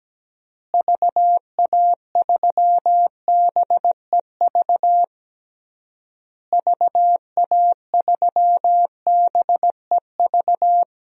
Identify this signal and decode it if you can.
Morse code, heard as a keyed tone, about 17 wpm, 700 Hz. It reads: VA3BEV VA3BEV
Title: DXCC-100 Call Signs - 6 Characters - Repeated 3x - 17wpm